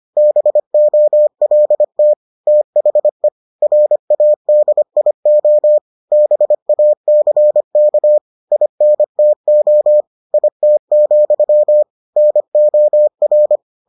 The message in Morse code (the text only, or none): BOLT THE RADIO BACK INTO IT, NOR